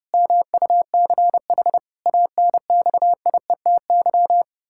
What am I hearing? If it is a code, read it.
Morse code, heard as a keyed tone, about 32 wpm, 700 Hz. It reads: MUCH ANXIETY
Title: Pangram Challenge v2 - 30wpm